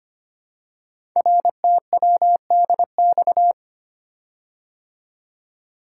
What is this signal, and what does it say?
Morse code, heard as a keyed tone, about 25 wpm, 700 Hz. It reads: RTWDX
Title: Code Groups - 5 Letters - 25wpm